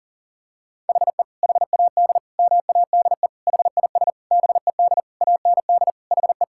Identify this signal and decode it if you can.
Morse code, heard as a keyed tone, about 40 wpm, 700 Hz. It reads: HE HAD MADE HIS BED AND HE
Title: Sentences from Top 1600 Words 40wpm